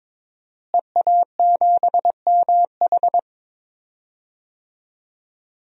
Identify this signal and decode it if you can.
Morse code, heard as a keyed tone, about 22 wpm, 700 Hz. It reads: EA7MH
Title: DXCC-100 Call Signs - 5 Characters - Repeated 2x - 22wpm